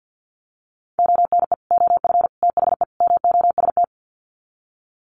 Morse code, heard as a keyed tone, about 50 wpm, 700 Hz. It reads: ONE OF THE MOST
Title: Sets of 4 Words 50wpm